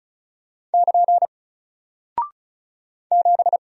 Morse code, heard as a keyed tone, about 35 wpm, 700 Hz. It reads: <KN> 7
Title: Single Character (Letter:Number:Punctuation:Prosign) 35wpm